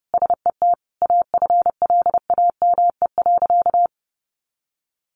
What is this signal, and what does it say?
Morse code, heard as a keyed tone, about 30 wpm, 700 Hz. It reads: SET AFLAME.
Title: A Sound Of Thunder 30wpm